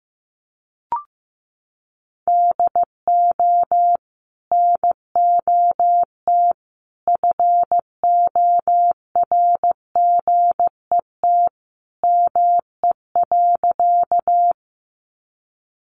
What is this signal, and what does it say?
Morse code, heard as a keyed tone, about 15 wpm, 700 Hz. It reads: DO NOT FORGET ME.
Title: Sentences from Top 1200 Words 15wpm